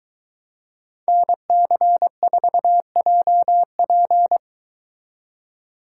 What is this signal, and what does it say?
Morse code, heard as a keyed tone, about 23 wpm, 700 Hz. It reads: NC4JP